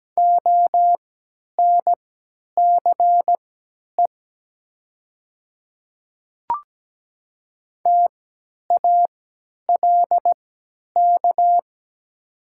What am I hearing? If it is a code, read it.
Morse code, heard as a keyed tone, about 17 wpm, spaced as for 10 wpm, 700 Hz. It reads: ONCE TALK
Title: 3x Character Spacing - Top 500 Words - 17wpm